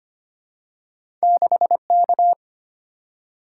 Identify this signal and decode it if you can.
Morse code, heard as a keyed tone, about 25 wpm, 700 Hz. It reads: 6K
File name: Two Letter-Number Combinations - 25wpm